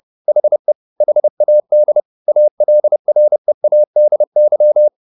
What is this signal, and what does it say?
Morse code, heard as a keyed tone, about 30 wpm, 600 Hz. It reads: HE HAD ALREADY